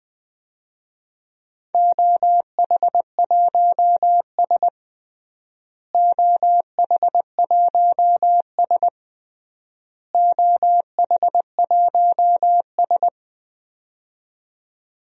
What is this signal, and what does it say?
Morse code, heard as a keyed tone, about 20 wpm, 700 Hz. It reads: OH1S OH1S OH1S
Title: DXCC-100 Call Signs - 4 Characters - Repeated 3x - 20wpm